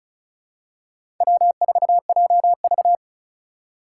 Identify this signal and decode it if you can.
Morse code, heard as a keyed tone, about 36 wpm, 700 Hz. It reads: W4JV
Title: USA Call Signs - 4 Characters - 2x Character Spacing - 35wpm